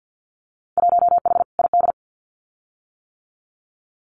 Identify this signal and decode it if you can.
Morse code, heard as a keyed tone, about 50 wpm, 700 Hz. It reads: 1H ID
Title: Winter Field Day Exchanges 50wpm